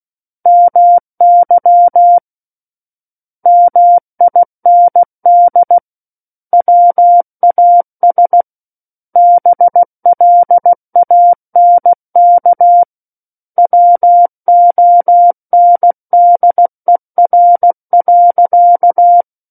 Morse code, heard as a keyed tone, about 16 wpm, 700 Hz. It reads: MY MIND WAS BLANK WONDER.
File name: Book One / 07-Chapter 7 - 16 WPM